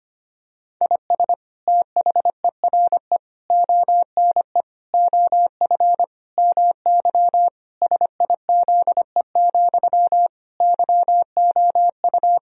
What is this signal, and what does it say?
Morse code, heard as a keyed tone, about 25 wpm, 700 Hz. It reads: IS THERE ONE OF MY SIZE, YOU
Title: Sentences from Top 1000 Words - Review Part 2 - 25wpm